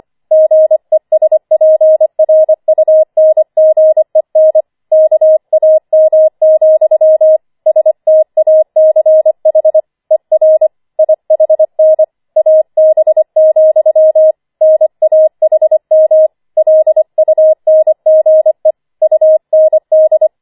Morse code, heard as a keyed tone, about 25 wpm, 630 Hz. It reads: GESPRUNGEN KAM, STACH ER IHN AB, NAHM LUNGE UND
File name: 25wpm